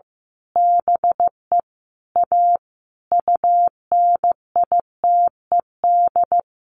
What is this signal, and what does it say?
Morse code, heard as a keyed tone, about 15 wpm, 700 Hz. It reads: BE A UNITED
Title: A Sound Of Thunder 15wpm